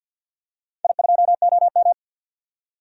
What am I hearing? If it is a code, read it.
Morse code, heard as a keyed tone, about 51 wpm, 700 Hz. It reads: I2OM
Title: DXCC-100 Call Signs - 4 Characters - 50wpm